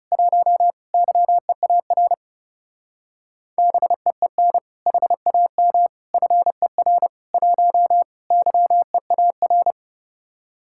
Morse code, heard as a keyed tone, about 34 wpm, 700 Hz. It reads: 1 YEAR BEEN HAM FER 1 YEAR
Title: Speed-Racing Common QSO Phrases Encore - 30wpm